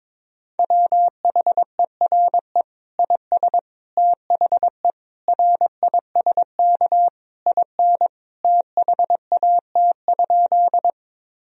Morse code, heard as a keyed tone, about 22 wpm, 700 Hz. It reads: WHERE IS THE RISK IN THAT?
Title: Sentences from Top 1000 Words - Review Part 4 - 22wpm